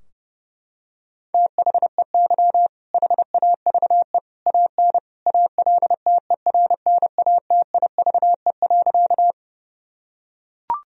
Morse code, heard as a keyed tone, about 30 wpm, 700 Hz. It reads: THEY HAVE AN ALTERNATIVE.
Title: Sentences from Top 1000 Words - Review Part 4 - 30wpm